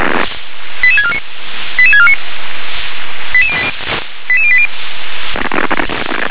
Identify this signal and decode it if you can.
Five-tone selective call (ZVEI), heard as quick runs of five tones, tones of about 70 ms. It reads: 88649 88649 88080